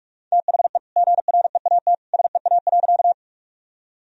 Morse code, heard as a keyed tone, about 46 wpm, 700 Hz. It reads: THE GREAT SEA.
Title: Sentences from Top 1000 Words - Review Part 1 - 45wpm